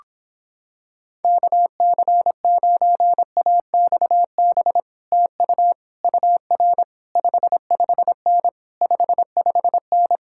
Morse code, heard as a keyed tone, about 26 wpm, 700 Hz. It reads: KC9AXB TU UR 55N 55N